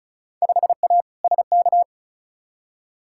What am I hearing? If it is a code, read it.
Morse code, heard as a keyed tone, about 36 wpm, 700 Hz. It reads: 5A SK
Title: Field Day Exchanges Encore - 35wpm